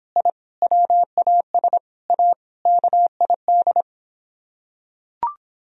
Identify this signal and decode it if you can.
Morse code, heard as a keyed tone, about 26 wpm, 700 Hz. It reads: I WAS A KID